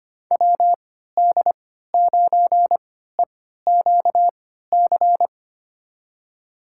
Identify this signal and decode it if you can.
Morse code, heard as a keyed tone, about 25 wpm, spaced as for 14 wpm, 700 Hz. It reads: WD9EQC
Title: USA Call Signs - 6 Characters - 3x Character Spacing - 25wpm